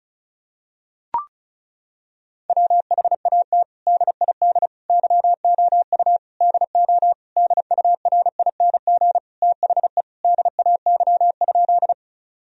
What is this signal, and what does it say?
Morse code, heard as a keyed tone, about 36 wpm, 700 Hz. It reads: WHAT DID YOU DO DURING THE DAY?
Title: Sentences from Top 1000 Words - Review Part 2 - 35wpm